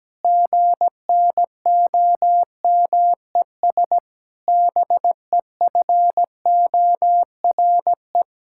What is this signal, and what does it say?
Morse code, heard as a keyed tone, about 17 wpm, 700 Hz. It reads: GNOMES BEFORE
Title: Pangram Challenge v2 - 17wpm